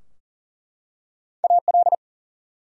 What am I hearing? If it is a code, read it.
Morse code, heard as a keyed tone, about 40 wpm, 700 Hz. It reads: AL
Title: USA Common Names - Max 4 Letters - 40wpm